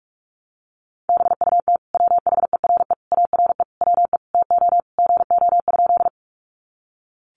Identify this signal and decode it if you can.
Morse code, heard as a keyed tone, about 44 wpm, 700 Hz. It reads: BUT WHERE ARE WE TO GO?
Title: Sentences from Top 200 Words 45wpm